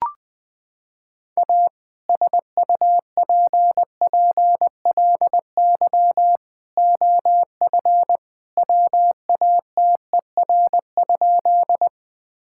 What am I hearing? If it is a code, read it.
Morse code, heard as a keyed tone, about 20 wpm, 700 Hz. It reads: A SUPPLY OF WATER?
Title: Sentences from Top 1000 Words - Review Part 5 - 20wpm